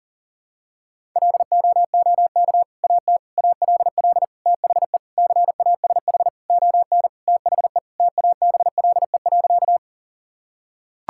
Morse code, heard as a keyed tone, about 40 wpm, 700 Hz. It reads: LOOK AT ALL THE CASH ON THE TABLE.
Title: Sentences from Top 1000 Words - Review Part 5 - 40wpm